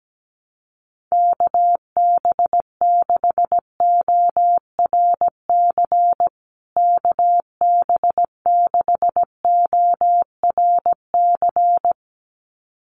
Morse code, heard as a keyed tone, about 17 wpm, 700 Hz. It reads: KB6ORC KB6ORC
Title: USA Call Signs Repeated 2x 17wpm